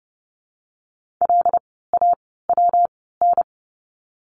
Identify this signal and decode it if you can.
Morse code, heard as a keyed tone, about 30 wpm, spaced as for 17 wpm, 700 Hz. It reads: LAWN
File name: course / 30:10wpm / 115 Lesson 19f - Letter w - Words 30wpm